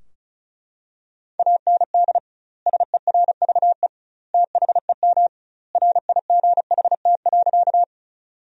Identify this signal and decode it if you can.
Morse code, heard as a keyed tone, about 34 wpm, 700 Hz. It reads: AND SERVE THEM RIGHT.
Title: 2x Word-Spacing - Sentences from Top 400 Words - 35wpm